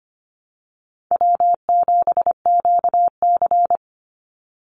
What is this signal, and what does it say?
Morse code, heard as a keyed tone, about 25 wpm, 700 Hz. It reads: W7QC
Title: Easy Call Signs Repeated 3x - 25wpm